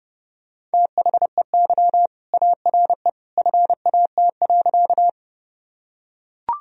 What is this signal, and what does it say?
Morse code, heard as a keyed tone, about 30 wpm, 700 Hz. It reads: THEY ARE FAT.
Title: Sentences from Top 1000 Words - Review Part 4 - 30wpm